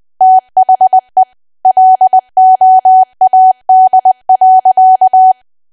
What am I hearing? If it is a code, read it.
Morse code, heard as a keyed tone, about 20 wpm, 750 Hz. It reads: THE LOAD.
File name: cwa / W1AW-Level3 / 20WPM